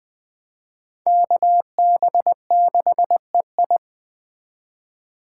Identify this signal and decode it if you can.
Morse code, heard as a keyed tone, about 20 wpm, 700 Hz. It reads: KB6EI
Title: Speed Racing - USA 2x2 Call Signs 17wpm